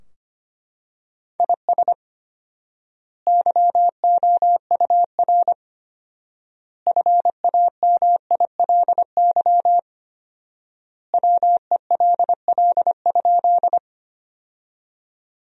Morse code, heard as a keyed tone, about 25 wpm, 700 Hz. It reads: IS YOUR FAMILY WELL?
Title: 4x Word-Spacing - Sentences from Top 400 Words - 25wpm